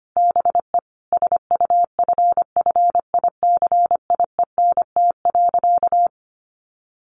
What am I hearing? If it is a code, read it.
Morse code, heard as a keyed tone, about 25 wpm, 700 Hz. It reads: BE SUFFICIENT.
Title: Sentences from Top 1400 Words 25wpm